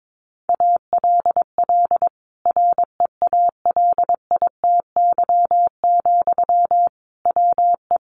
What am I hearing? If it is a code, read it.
Morse code, heard as a keyed tone, about 22 wpm, 700 Hz. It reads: ALL REALITY, WE
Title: Sentences from Top 1400 Words 22wpm